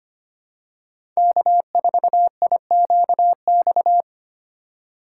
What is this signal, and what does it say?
Morse code, heard as a keyed tone, about 25 wpm, 700 Hz. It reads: K4IQX
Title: USA Call Signs - 5 Characters - 8x Character Spacing - 25wpm